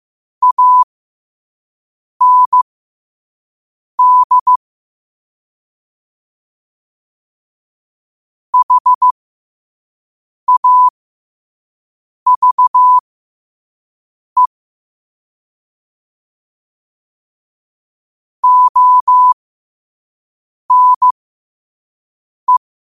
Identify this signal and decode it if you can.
Morse code, heard as a keyed tone, about 15 wpm, spaced as for 5 wpm, 1000 Hz. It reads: AND HAVE ONE
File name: qotdfiles / 5wpm